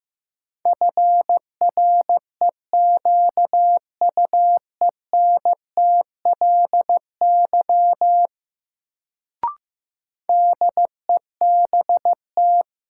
Morse code, heard as a keyed tone, about 15 wpm, 700 Hz. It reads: FREQUENTLY DEBT
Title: Sentences from Top 1300 Words - New Word Drill Alt - 15wpm